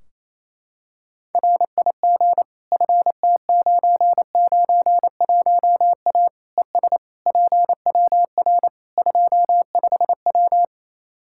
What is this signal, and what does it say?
Morse code, heard as a keyed tone, about 27 wpm, 700 Hz. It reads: RIG FT991A ES PWR 25W
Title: Practice QSO - Exchange 2 - Info - 28wpm